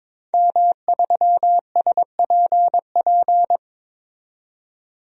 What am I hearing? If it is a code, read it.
Morse code, heard as a keyed tone, about 22 wpm, 700 Hz. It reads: M3SPP
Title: DXCC-100 Call Signs - 5 Characters - 22wpm